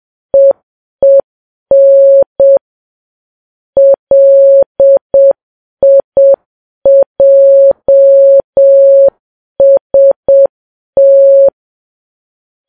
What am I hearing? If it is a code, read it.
Morse code, heard as a keyed tone, about 7 wpm, 550 Hz. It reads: EEN LIJST
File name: Files Web tls / Morse on9fv / morse7kps